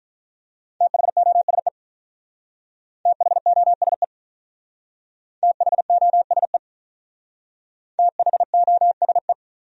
Morse code, heard as a keyed tone, about 46 wpm, 700 Hz. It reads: THOSE THOSE THOSE THOSE